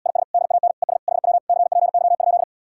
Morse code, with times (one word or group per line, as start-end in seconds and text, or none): 0.1-0.2 IS
0.3-0.7 DAN
0.8-1.0 ES
1.1-1.4 HW
1.5-2.4 CPY?